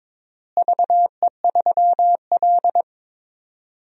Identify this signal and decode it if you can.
Morse code, heard as a keyed tone, about 22 wpm, 700 Hz. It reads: VE3L
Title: DXCC-100 Call Signs - 4 Characters - Repeated 3x - 22wpm